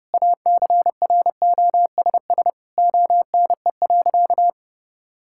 Morse code, heard as a keyed tone, about 32 wpm, 700 Hz. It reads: ACROSS ONE.
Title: Sentences from Top 1000 Words - Review Part 5 - 30wpm